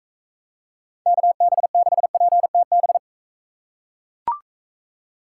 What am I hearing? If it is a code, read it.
Morse code, heard as a keyed tone, about 42 wpm, 700 Hz. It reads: KB6PTB